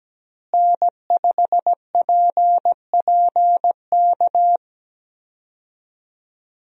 Morse code, heard as a keyed tone, about 17 wpm, 700 Hz. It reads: N5PPK
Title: USA Call Signs - 5 Characters - Repeated 3x - 17wpm